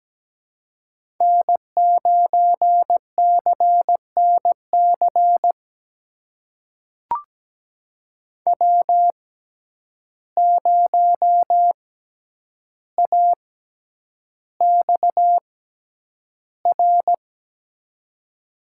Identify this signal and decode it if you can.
Morse code, heard as a keyed tone, about 17 wpm, spaced as for 9 wpm, 700 Hz. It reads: N9CNC W0AXR
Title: USA Call Signs - 5 Characters - 6x Character Spacing - 17wpm